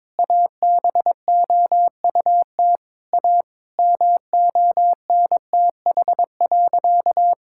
Morse code, heard as a keyed tone, about 22 wpm, 700 Hz. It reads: ABOUT A MONTH.
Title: Sentences from Top 1000 Words - Review Part 6 - 22wpm